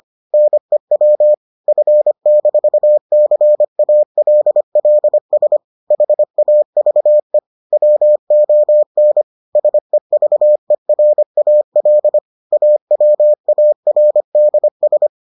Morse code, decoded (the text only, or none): NEW F-CALLS HAVE WON SEVERAL AWARDS